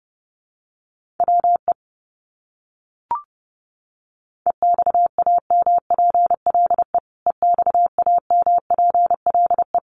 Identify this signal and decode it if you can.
Morse code, heard as a keyed tone, about 30 wpm, 700 Hz. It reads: WE EXAMPLE EXAMPLE
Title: Top 300 Words Repeated 30wpm